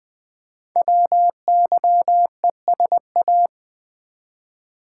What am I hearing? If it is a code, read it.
Morse code, heard as a keyed tone, about 21 wpm, 700 Hz. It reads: WYESA
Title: Code Groups - 5 Letters - 20wpm